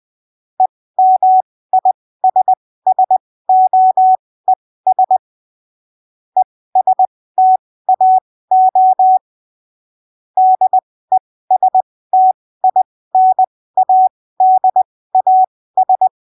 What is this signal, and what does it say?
Morse code, heard as a keyed tone, about 20 wpm, spaced as for 15 wpm, 750 Hz. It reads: EMISSOES ESTAO DESTINADAS
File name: arquivos / texto / texto 01 20ppm